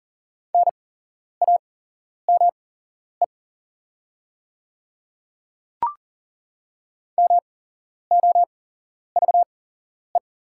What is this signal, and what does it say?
Morse code, heard as a keyed tone, about 38 wpm, spaced as for 11 wpm, 690 Hz. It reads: NAME MOVE
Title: 8x Character Spacing - Top 300 Words - 40wpm